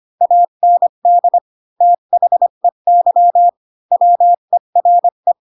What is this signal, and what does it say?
Morse code, heard as a keyed tone, about 25 wpm, 700 Hz. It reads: AND THEY WERE
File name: Pinocchio / Pinocchio ch3 20 25 ik1hge